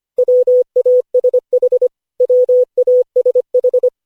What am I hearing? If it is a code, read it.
Morse code, heard as a keyed tone, about 25 wpm, 500 Hz. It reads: WASH WASH
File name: cw / Session5 cw training